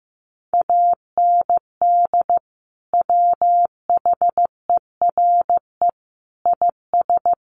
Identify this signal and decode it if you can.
Morse code, heard as a keyed tone, about 14 wpm, 700 Hz. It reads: AND WHERE IS
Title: Sentences from Top 1300 Words 15wpm